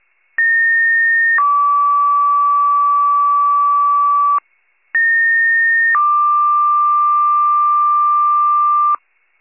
Two-tone page, heard as a one-second tone, then a three-second tone, about 1850 then 1150 Hz, about 1850 then 1180 Hz. It is heard from 0.4 to 9.0 s.
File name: wcalert